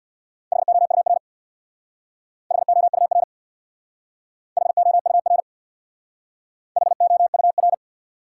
Morse code, heard as a keyed tone, about 62 wpm, 700 Hz. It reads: HOUR HOUR HOUR HOUR
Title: Speed-Racing Top 180 QSO Elements 50wpm